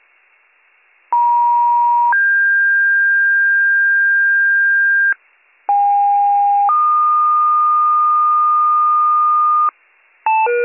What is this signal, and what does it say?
Two-tone page, heard as a one-second tone, then a three-second tone, about 950 then 1600 Hz, about 810 then 1220 Hz, with noise. It is heard from 1.1 to 9.7 s.